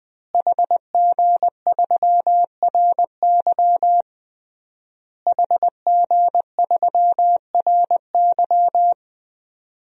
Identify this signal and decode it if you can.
Morse code, heard as a keyed tone, about 21 wpm, 700 Hz. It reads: HG3RY HG3RY
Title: DXCC-100 Call Signs - 5 Characters - Repeated 3x - 20wpm